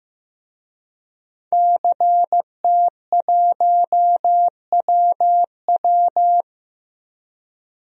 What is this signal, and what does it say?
Morse code, heard as a keyed tone, about 14 wpm, 700 Hz. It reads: CT1WW
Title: DXCC-100 Call Signs - 5 Characters - Repeated 3x - 15wpm